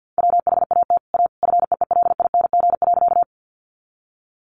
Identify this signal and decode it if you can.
Morse code, heard as a keyed tone, about 48 wpm, 700 Hz. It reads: WHAT A FEELING.
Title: Sentences from Top 1200 Words 50wpm